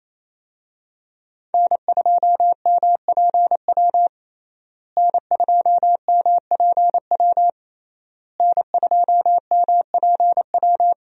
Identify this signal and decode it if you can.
Morse code, heard as a keyed tone, about 28 wpm, 700 Hz. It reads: N2MPW N2MPW N2MPW
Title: USA Call Signs - 5 Characters - Repeated 3x - 28wpm